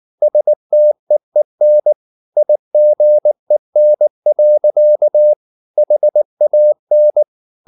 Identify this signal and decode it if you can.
Morse code, heard as a keyed tone, about 19 wpm, 600 Hz. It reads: STEEN IGEN. HAN